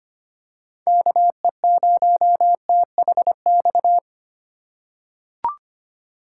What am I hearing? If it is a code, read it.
Morse code, heard as a keyed tone, about 25 wpm, 700 Hz. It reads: KE0THX
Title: USA Call Signs - 6 Characters - 4x Character Spacing - 25wpm